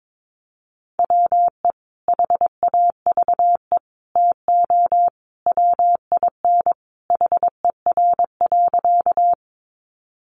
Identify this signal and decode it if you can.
Morse code, heard as a keyed tone, about 22 wpm, 700 Hz. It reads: WE HAVE TO WIN HER.
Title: Sentences from Top 1500 Words 22wpm